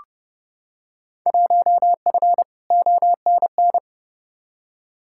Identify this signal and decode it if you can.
Morse code, heard as a keyed tone, about 32 wpm, 700 Hz. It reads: 1F ONN
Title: Field Day Exchanges Encore - 30wpm